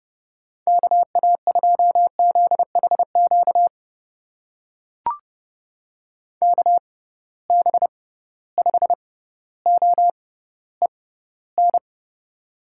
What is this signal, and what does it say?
Morse code, heard as a keyed tone, about 30 wpm, spaced as for 15 wpm, 700 Hz. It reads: KA2ZHQ KB5OEN
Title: USA Call Signs - 6 Characters - 6x Character Spacing - 30wpm